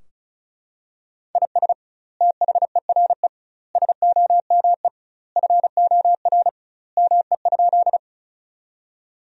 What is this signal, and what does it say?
Morse code, heard as a keyed tone, about 36 wpm, 700 Hz. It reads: IS THERE SOME FOR ME?
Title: 2x Word-Spacing - Sentences from Top 200 Words - 35wpm